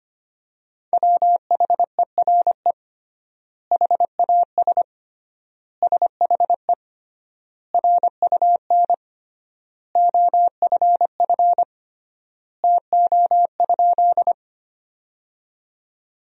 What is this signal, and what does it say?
Morse code, heard as a keyed tone, about 25 wpm, 700 Hz. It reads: WHERE HAS SHE RUN OFF TO?
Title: 3x Word-Spacing - Sentences from Top 300 Words - 25wpm